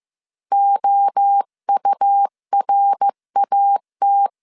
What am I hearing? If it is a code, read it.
Morse code, heard as a keyed tone, about 15 wpm, 800 Hz. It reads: OURAT
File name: CW / DET / 1 3 1500C S C 14wpm 2458